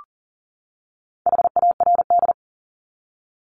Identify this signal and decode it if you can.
Morse code, heard as a keyed tone, about 40 wpm, 700 Hz. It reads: HARD
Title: LY Suffix - Top 100 Words with Base Words 40wpm